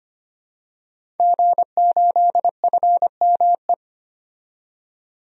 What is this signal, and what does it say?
Morse code, heard as a keyed tone, about 25 wpm, 700 Hz. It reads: G8FME
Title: DXCC-100 Call Signs - 5 Characters - 25wpm